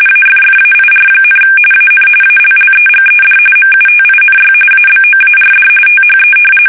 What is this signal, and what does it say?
RTTY, about 45.45 baud. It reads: AFFAIR, AN ALLEGED MUTINY ABOARD THE
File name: fsk850 / usnavy2